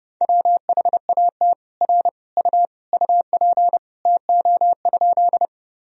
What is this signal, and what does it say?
Morse code, heard as a keyed tone, about 30 wpm, 700 Hz. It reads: WHAT R U UP TO?
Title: Sentences from Top 100 Words and Abbreviations 30wpm